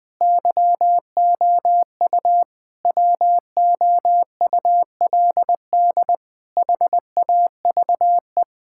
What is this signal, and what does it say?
Morse code, heard as a keyed tone, about 20 wpm, 700 Hz. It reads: YOU WOULD HAVE
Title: Sets of 9 Words - 20wpm